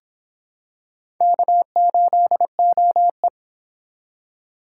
Morse code, heard as a keyed tone, about 26 wpm, 700 Hz. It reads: K8OE